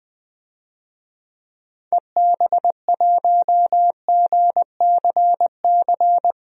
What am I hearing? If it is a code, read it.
Morse code, heard as a keyed tone, about 21 wpm, 700 Hz. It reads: EB1GCC
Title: DXCC-100 Call Signs - 6 Characters - 20wpm